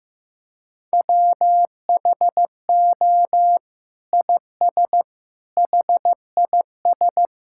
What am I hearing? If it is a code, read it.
Morse code, heard as a keyed tone, about 15 wpm, 700 Hz. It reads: WHO IS HIS
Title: Sentences from Top 1600 Words 15wpm